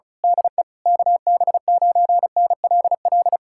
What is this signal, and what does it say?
Morse code, heard as a keyed tone, about 33 wpm, 700 Hz. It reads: DE KB9NLL
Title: Speed-Racing Common QSO Phrases - 35wpm